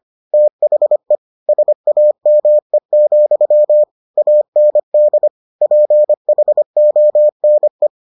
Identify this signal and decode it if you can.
Morse code, heard as a keyed tone, about 25 wpm, 600 Hz. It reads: THE SAME, AND PHONE